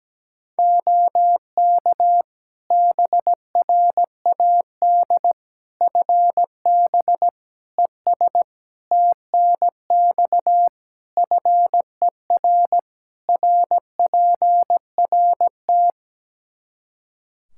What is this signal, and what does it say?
Morse code, heard as a keyed tone, about 17 wpm, 700 Hz. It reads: OK BRAD FB ES TNX FER RPRT
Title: K1USN SST QSO Practice 17wpm